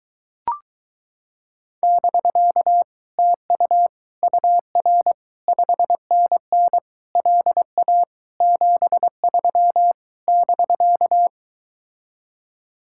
Morse code, heard as a keyed tone, about 23 wpm, 700 Hz. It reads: <BK> TU UR 5NN LA 73 <BK>